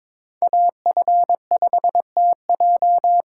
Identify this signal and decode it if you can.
Morse code, heard as a keyed tone, about 26 wpm, 700 Hz. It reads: AF5TJ
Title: Speed Racing - USA 2x2 Call Signs 22wpm